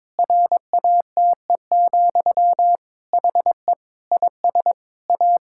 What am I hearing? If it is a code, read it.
Morse code, heard as a keyed tone, about 22 wpm, 700 Hz. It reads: RATE, HE IS A
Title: Sentences from Top 1000 Words - Review Part 5 - 22wpm